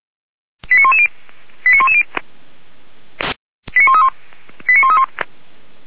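Five-tone selective call (ZVEI), heard as quick runs of five tones, tones of about 70 ms. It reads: 89110 89110 89131 89131
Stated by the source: Five-tone ZVEI selective call 89109, received by SDR.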